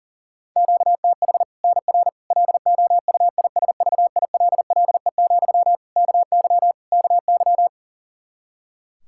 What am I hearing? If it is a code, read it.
Morse code, heard as a keyed tone, about 37 wpm, 700 Hz. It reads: QTH NR LOUISVILLE, KY KY
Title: K1USN SST QSO Practice 40wpm